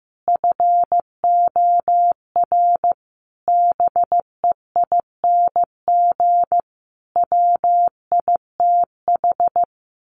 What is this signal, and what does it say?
Morse code, heard as a keyed tone, about 15 wpm, 700 Hz. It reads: FOR BEING WITH
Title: Sets of 5 Words Encore 15wpm